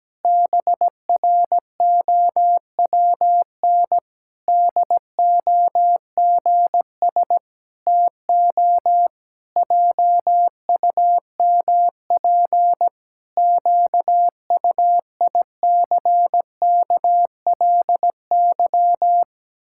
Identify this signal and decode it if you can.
Morse code, heard as a keyed tone, about 17 wpm, 700 Hz. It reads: BROWN DOGS TO JUMP QUICKLY
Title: Pangram Challenge v2 - 17wpm